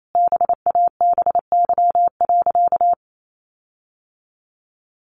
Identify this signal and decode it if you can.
Morse code, heard as a keyed tone, about 28 wpm, 700 Hz. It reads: BABY.
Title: Sentences from Top 1500 Words 28wpm